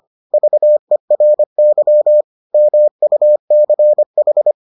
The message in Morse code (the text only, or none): VERY MUCH